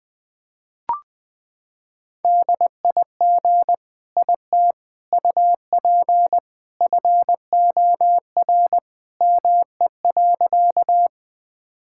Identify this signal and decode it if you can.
Morse code, heard as a keyed tone, about 20 wpm, 700 Hz. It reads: DIG IT UP FOR ME.
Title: Sentences from Top 1600 Words 20wpm